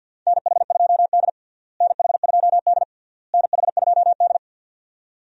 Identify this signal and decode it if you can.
Morse code, heard as a keyed tone, about 53 wpm, 700 Hz. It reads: NH2D NH2D NH2D
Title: USA Call Signs - 4 Characters - Repeated 3x - 50wpm